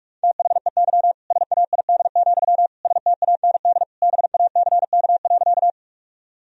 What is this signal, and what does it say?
Morse code, heard as a keyed tone, about 42 wpm, 700 Hz. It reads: THEY SAID, STAND BACK.
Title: Sentences from Top 1000 Words - Review Part 1 - 45wpm